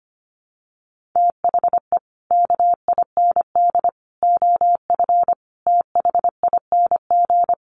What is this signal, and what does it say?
Morse code, heard as a keyed tone, about 25 wpm, 700 Hz. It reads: THE KIND OF THING
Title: Sets of 5 Words Encore 25wpm